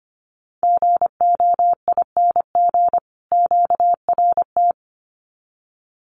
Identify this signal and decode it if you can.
Morse code, heard as a keyed tone, about 25 wpm, 700 Hz. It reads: GOING QRT
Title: Common QSO Phrases 25wpm